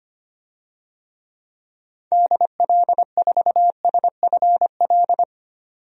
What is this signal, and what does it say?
Morse code, heard as a keyed tone, about 25 wpm, 690 Hz. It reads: DL4SFL
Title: DXCC-100 Call Signs - 6 Characters - 25wpm